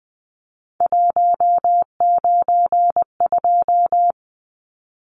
Morse code, heard as a keyed tone, about 20 wpm, 700 Hz. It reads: 192
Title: Numbers - 3-Digit 20wpm